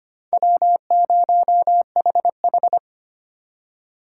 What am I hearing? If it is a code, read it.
Morse code, heard as a keyed tone, about 25 wpm, 700 Hz. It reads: W0HH
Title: USA Call Signs - 4 Characters - 2x Character Spacing - 25wpm